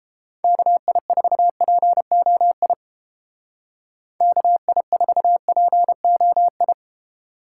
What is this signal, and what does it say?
Morse code, heard as a keyed tone, about 31 wpm, 700 Hz. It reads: KI4POI KI4POI